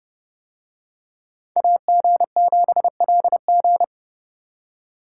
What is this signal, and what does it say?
Morse code, heard as a keyed tone, about 28 wpm, 700 Hz. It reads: AG7LG